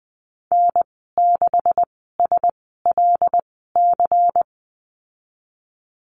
Morse code, heard as a keyed tone, about 20 wpm, spaced as for 15 wpm, 700 Hz. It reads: N6SLC
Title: 147 Lesson 24g - Letter y - Call Signs 20wpm